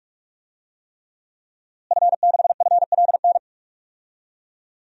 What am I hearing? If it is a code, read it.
Morse code, heard as a keyed tone, about 46 wpm, 700 Hz. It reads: F6FLN
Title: DXCC-100 Call Signs - 5 Characters - Repeated 3x - 45wpm